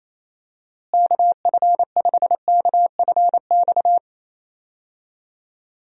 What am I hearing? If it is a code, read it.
Morse code, heard as a keyed tone, about 28 wpm, 700 Hz. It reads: KF5KFX
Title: USA Call Signs - 6 Characters - Repeated 3x - 28wpm